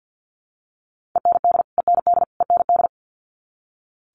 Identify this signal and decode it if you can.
Morse code, heard as a keyed tone, about 50 wpm, 700 Hz. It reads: END END END
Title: Top 300 Words Repeated 50wpm